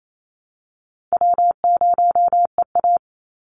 Morse code, heard as a keyed tone, about 28 wpm, 700 Hz. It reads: W0EA
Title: Easy Call Signs Repeated 2x - 28wpm